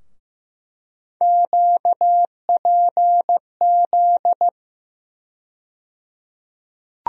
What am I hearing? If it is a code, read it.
Morse code, heard as a keyed tone, about 15 wpm, 700 Hz. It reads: QPZ
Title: USA Call Sign Postfixes - 3 Letters 15wpm